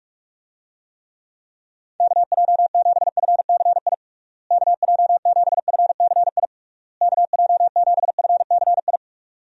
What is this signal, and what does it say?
Morse code, heard as a keyed tone, about 42 wpm, 700 Hz. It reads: KJ7FCI KJ7FCI KJ7FCI
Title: USA Call Signs - 6 Characters - Repeated 3x - 45wpm